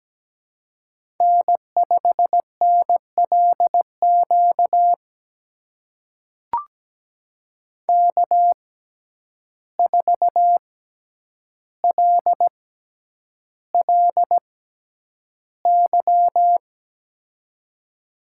Morse code, heard as a keyed tone, about 17 wpm, spaced as for 9 wpm, 700 Hz. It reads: N5NLQ K4LLY
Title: USA Call Signs - 5 Characters - 6x Character Spacing - 17wpm